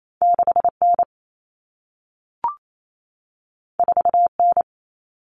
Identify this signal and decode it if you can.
Morse code, heard as a keyed tone, about 28 wpm, 700 Hz. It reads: 6N 4N
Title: Common Cut Numbers - 2-Digit 28wpm